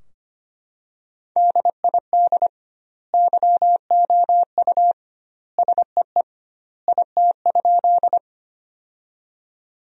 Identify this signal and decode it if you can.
Morse code, heard as a keyed tone, about 25 wpm, 700 Hz. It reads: DID YOU SEE IT?
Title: 2x Word-Spacing - Sentences from Top 100 Words - 25wpm